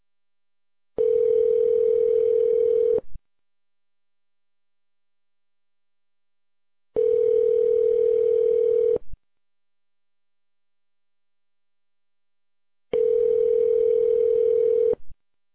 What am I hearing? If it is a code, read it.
Ringback tone (North America), 440 + 480 Hz.